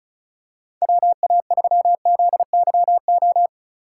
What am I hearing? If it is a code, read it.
Morse code, heard as a keyed tone, about 34 wpm, 700 Hz. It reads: WA3ZYO
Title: USA Call Signs - 6 Characters - 4x Character Spacing - 35wpm